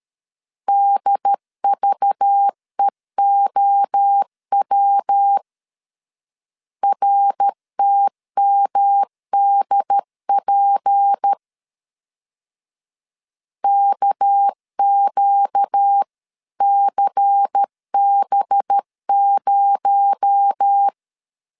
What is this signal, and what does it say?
Morse code, heard as a keyed tone, about 13 wpm, 800 Hz. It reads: DVEOW RTMDP KQCB0
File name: CW / DET / 1 6 1500C S B 12wpm 3431